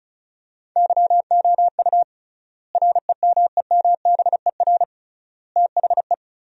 Morse code, heard as a keyed tone, about 34 wpm, 700 Hz. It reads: YOU REMEMBER THE
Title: 3x Word-Spacing - Sentences from Top 400 Words - 35wpm